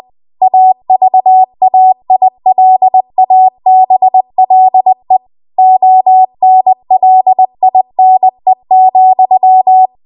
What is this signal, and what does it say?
Morse code, heard as a keyed tone, about 20 wpm, 750 Hz. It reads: AVAILABLE ONLINE,
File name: Archive / 20 WPM / 20WPM